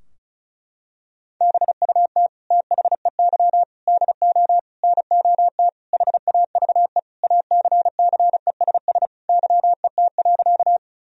Morse code, heard as a keyed tone, about 34 wpm, 700 Hz. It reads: BUT THEY DO NOT HAVE ACCESS YET.
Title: Sentences from Top 1000 Words - Review Part 4 - 35wpm